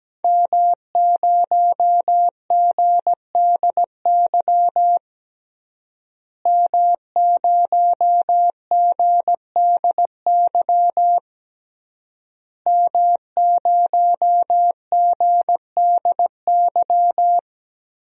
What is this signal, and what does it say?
Morse code, heard as a keyed tone, about 17 wpm, 700 Hz. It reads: M0GDY M0GDY M0GDY
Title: DXCC-100 Call Signs - 5 Characters - Repeated 3x - 17wpm